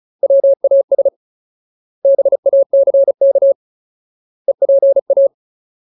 Morse code, heard as a keyed tone, about 34 wpm, 550 Hz. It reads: WAS BACK EPA
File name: Ultimate Word List / 35WPM Ultimate